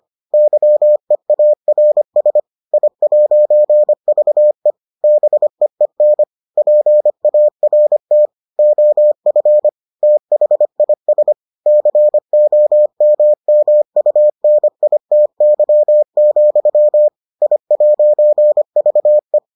Morse code, heard as a keyed tone, about 25 wpm, 600 Hz. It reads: YEARS I'VE BEEN PART OF THIS COMMUNITY, I'VE